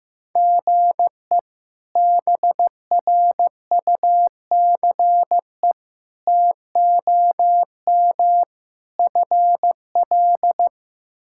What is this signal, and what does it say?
Morse code, heard as a keyed tone, about 15 wpm, 700 Hz. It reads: GE BRUCE TOM FL
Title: K1USN SST QSO Practice - 15wpm